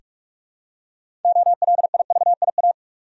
Morse code, heard as a keyed tone, about 46 wpm, 700 Hz. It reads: OLIVIA
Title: USA Common Names - 45wpm